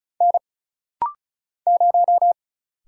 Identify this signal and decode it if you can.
Morse code, heard as a keyed tone, about 35 wpm, 700 Hz. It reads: N 0
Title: Single Letter-Number - Rapid-Fire 35wpm